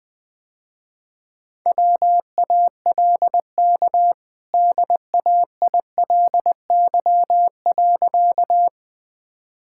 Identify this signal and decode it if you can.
Morse code, heard as a keyed tone, about 20 wpm, 700 Hz. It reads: WALK DAILY.
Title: Easy Sentences with Hints - 20wpm